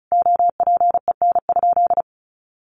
Morse code, heard as a keyed tone, about 34 wpm, 700 Hz. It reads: OPEN?
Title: Sentences from Top 1300 Words 35wpm